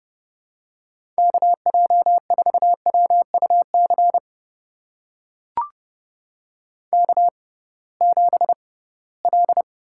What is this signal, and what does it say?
Morse code, heard as a keyed tone, about 30 wpm, spaced as for 19 wpm, 700 Hz. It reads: KJ4WUC K7L
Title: USA Call Signs - 6x Character Spacing - 30wpm